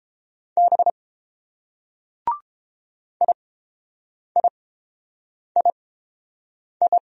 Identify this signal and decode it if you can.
Morse code, heard as a keyed tone, about 28 wpm, spaced as for 7 wpm, 700 Hz. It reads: B IIII